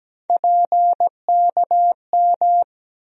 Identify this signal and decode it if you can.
Morse code, heard as a keyed tone, about 17 wpm, 700 Hz. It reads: PKM
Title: USA Call Sign Postfixes - 3 Letters 17wpm